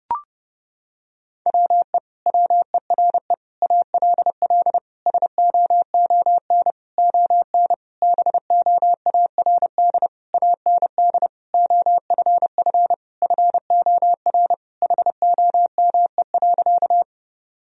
Morse code, heard as a keyed tone, about 30 wpm, 700 Hz. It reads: WE WERE ALL SOON ON BOARD AND OFF FOR HOME.
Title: Sentences from Top 1000 Words - Review Part 4 - 30wpm